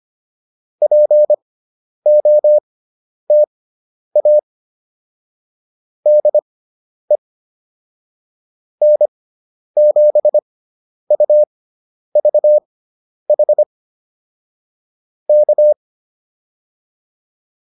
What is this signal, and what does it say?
Morse code, heard as a keyed tone, about 25 wpm, spaced as for 10 wpm, 600 Hz. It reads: POTA DE N7UVH K
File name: POTA204 10